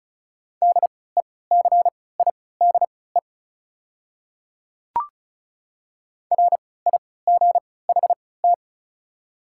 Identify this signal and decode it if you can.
Morse code, heard as a keyed tone, about 35 wpm, spaced as for 20 wpm, 690 Hz. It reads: DECIDE RIGHT
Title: 3x Character Spacing - Top 500 Words - 35wpm